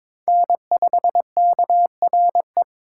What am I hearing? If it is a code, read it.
Morse code, heard as a keyed tone, about 26 wpm, 700 Hz. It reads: N5KRE
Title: Speed Racing - USA 1x3 Call Signs 22wpm